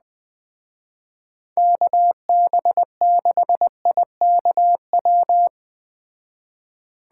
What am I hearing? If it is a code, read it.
Morse code, heard as a keyed tone, about 20 wpm, 700 Hz. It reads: KB6IKW